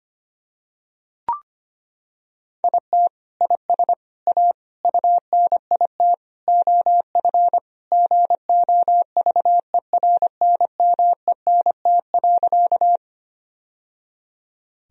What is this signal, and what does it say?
Morse code, heard as a keyed tone, about 25 wpm, 700 Hz. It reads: IT IS A UNIT OF GOVERNMENT.
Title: Sentences from Top 1000 Words - Review Part 2 - 25wpm